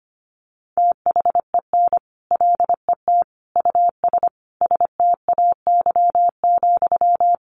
Morse code, heard as a keyed tone, about 25 wpm, 700 Hz. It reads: THEN LET US STAY,
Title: Sentences from Top 1500 Words 25wpm